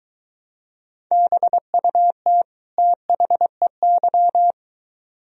Morse code, heard as a keyed tone, about 23 wpm, 700 Hz. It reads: BUT THEY